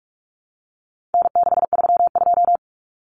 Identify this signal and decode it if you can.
Morse code, heard as a keyed tone, about 44 wpm, 700 Hz. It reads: N632
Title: Common Cut Numbers - 4-Digit 45wpm